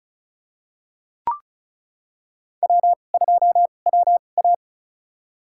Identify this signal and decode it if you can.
Morse code, heard as a keyed tone, about 36 wpm, spaced as for 25 wpm, 690 Hz. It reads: W2WA
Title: USA Call Signs - 4 Characters - 2x Character Spacing - 35wpm